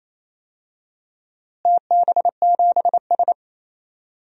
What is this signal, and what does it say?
Morse code, heard as a keyed tone, about 28 wpm, 700 Hz. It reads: TB7S
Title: DXCC-100 Call Signs - 4 Characters - Repeated 3x - 28wpm